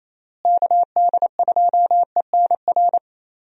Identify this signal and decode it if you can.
Morse code, heard as a keyed tone, about 28 wpm, 700 Hz. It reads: KD2ENR
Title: USA Call Signs - 6 Characters - Repeated 2x - 28wpm